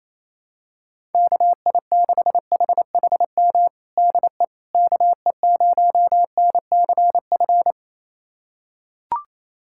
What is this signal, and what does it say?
Morse code, heard as a keyed tone, about 28 wpm, 700 Hz. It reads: KI6HHM DE KE0NCF
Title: USA Callsign DE Callsign 28wpm